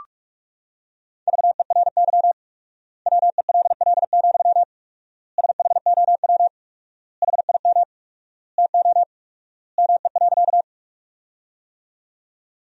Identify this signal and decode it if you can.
Morse code, heard as a keyed tone, about 47 wpm, 700 Hz. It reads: VERY WELL, SHOW HIM TO ME.
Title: 4x Word-Spacing - Sentences from Top 200 Words - 45wpm